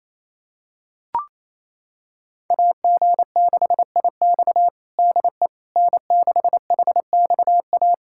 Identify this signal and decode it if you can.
Morse code, heard as a keyed tone, about 28 wpm, 700 Hz. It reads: AG6IX DE N6HXA
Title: USA Callsign DE Callsign - 5 Characters - 28wpm